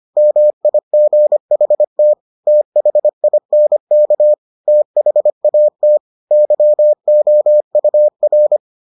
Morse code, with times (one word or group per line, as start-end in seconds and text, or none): 0.2-2.1 MIGHT
2.5-4.3 THINK
4.7-6.0 THAT
6.3-8.6 YOUR